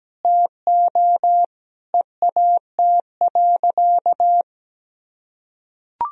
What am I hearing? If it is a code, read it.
Morse code, heard as a keyed tone, about 17 wpm, 700 Hz. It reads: TO EAT.
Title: Sentences from Top 1000 Words - Review Part 3 - 17wpm